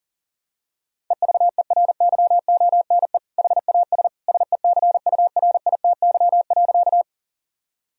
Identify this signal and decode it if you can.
Morse code, heard as a keyed tone, about 40 wpm, 700 Hz. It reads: EVERYONE HAS SECURITY.
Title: Sentences from Top 1000 Words - Review Part 4 - 40wpm